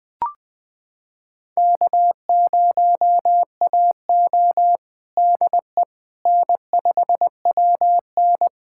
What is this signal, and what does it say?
Morse code, heard as a keyed tone, about 21 wpm, 700 Hz. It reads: K0AO DE N5WN
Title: USA Callsign DE Callsign - 4 Characters - 20wpm